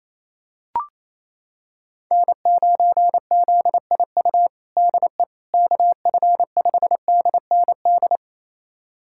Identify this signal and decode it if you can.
Morse code, heard as a keyed tone, about 28 wpm, 700 Hz. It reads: N9ZIU DE KF5DND
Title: USA Callsign DE Callsign 28wpm